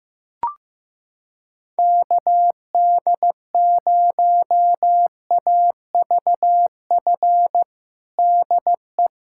Morse code, heard as a keyed tone, about 15 wpm, 700 Hz. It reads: KD0AVF DE
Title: USA Callsign DE Callsign 15wpm